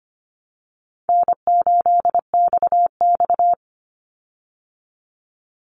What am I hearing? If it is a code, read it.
Morse code, heard as a keyed tone, about 25 wpm, 700 Hz. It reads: N8XX
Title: Call Signs Repeated 2x - 25wpm